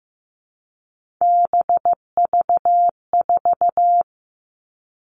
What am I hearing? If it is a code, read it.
Morse code, heard as a keyed tone, about 15 wpm, 700 Hz. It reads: BV4
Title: All Cut Numbers - 3-Digit 15wpm